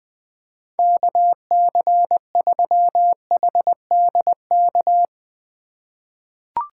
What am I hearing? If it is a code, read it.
Morse code, heard as a keyed tone, about 20 wpm, 700 Hz. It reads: KC3HDK
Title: USA Call Signs - 6 Characters - 2x Character Spacing - 20wpm